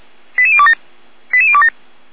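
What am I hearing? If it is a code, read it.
Five-tone selective call (ZVEI), heard as quick runs of five tones, tones of about 70 ms. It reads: 88027 88027
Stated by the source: Five-tone ZVEI selective call received on SDR.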